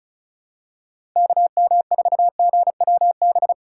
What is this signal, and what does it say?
Morse code, heard as a keyed tone, about 36 wpm, 700 Hz. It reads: KM4GWB
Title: USA Call Signs - 6 Characters - 3x Character Spacing - 35wpm